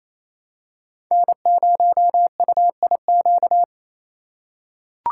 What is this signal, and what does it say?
Morse code, heard as a keyed tone, about 28 wpm, 700 Hz. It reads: N0UIQ
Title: USA Call Signs - 5 Characters - 2x Character Spacing - 28wpm